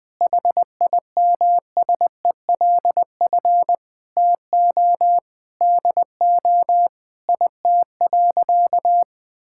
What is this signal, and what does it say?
Morse code, heard as a keyed tone, about 20 wpm, 700 Hz. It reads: HIMSELF TO DO IT.
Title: Sentences from Top 1600 Words 20wpm